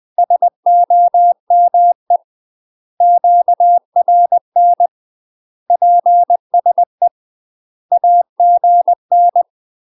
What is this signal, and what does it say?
Morse code, heard as a keyed tone, about 20 wpm, 700 Hz. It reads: SOME QRN PSE AGN
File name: UK MASTER QSOs TRAINING / 20WPM / UK MASTER QSOs 0056